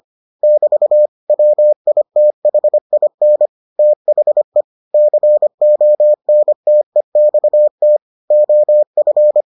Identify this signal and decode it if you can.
Morse code, heard as a keyed tone, about 25 wpm, 600 Hz. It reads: <BT> WITHIN THE CONTEXT OF